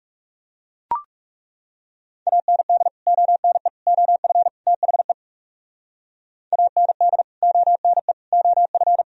Morse code, heard as a keyed tone, about 40 wpm, 700 Hz. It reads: AND ONE OF THE AND ONE OF